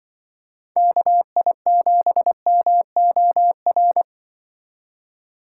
Morse code, heard as a keyed tone, about 25 wpm, 700 Hz. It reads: KI7MOR
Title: Speed Racing - USA 2x3 Call Signs 20wpm